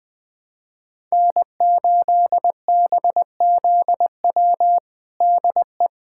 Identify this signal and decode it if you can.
Morse code, heard as a keyed tone, about 20 wpm, 700 Hz. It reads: N8BZW DE
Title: USA Callsign DE Callsign 20wpm